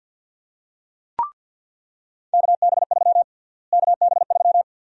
Morse code, heard as a keyed tone, about 47 wpm, 700 Hz. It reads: KB3 KB3
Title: USA Call Sign Prefixes - Repeated 2x - 50wpm